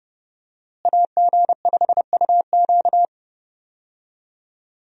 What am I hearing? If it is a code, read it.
Morse code, heard as a keyed tone, about 30 wpm, 700 Hz. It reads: AG5UQ
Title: USA Call Signs - 5 Characters - Repeated 3x - 30wpm